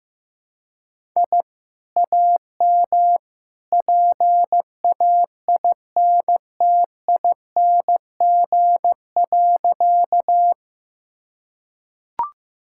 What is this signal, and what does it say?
Morse code, heard as a keyed tone, about 15 wpm, 700 Hz. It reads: I AM PAINTING.
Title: Sentences from Top 1000 Words - Review Part 6 - 15wpm